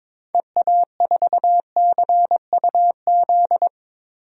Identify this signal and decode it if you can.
Morse code, heard as a keyed tone, about 21 wpm, 700 Hz. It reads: EA4CUZ
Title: DXCC-100 Call Signs - 6 Characters - 22wpm